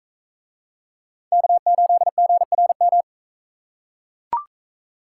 Morse code, heard as a keyed tone, about 39 wpm, 700 Hz. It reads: K8GRM